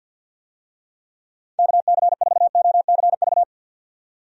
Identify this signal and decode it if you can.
Morse code, heard as a keyed tone, about 51 wpm, 700 Hz. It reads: KC4OCV
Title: USA Call Signs - 6 Characters - 50wpm